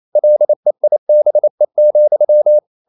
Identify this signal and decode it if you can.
Morse code, heard as a keyed tone, about 28 wpm, 600 Hz. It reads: LEIBE,
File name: Liebespaar4 28wpm